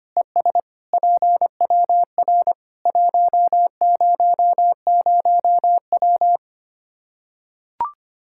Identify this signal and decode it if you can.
Morse code, heard as a keyed tone, about 25 wpm, 700 Hz. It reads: ES PWR 100W
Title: Practice QSO - Exchange 2 - Info - 25wpm